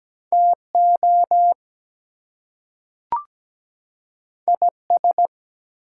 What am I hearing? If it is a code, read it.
Morse code, heard as a keyed tone, about 17 wpm, 700 Hz. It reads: TO IS
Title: Two Letter Words - 17wpm